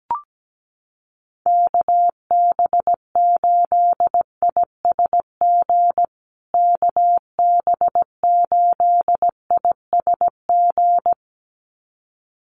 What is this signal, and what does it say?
Morse code, heard as a keyed tone, about 17 wpm, 700 Hz. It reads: KB8ISG KB8ISG
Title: USA Call Signs Repeated 2x 17wpm